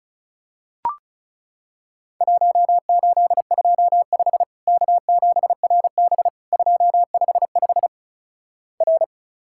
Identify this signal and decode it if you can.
Morse code, heard as a keyed tone, about 36 wpm, 700 Hz. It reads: 1825 K7RB 255 R
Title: Worked All Europe DX Contest - QTC Exchange - 35wpm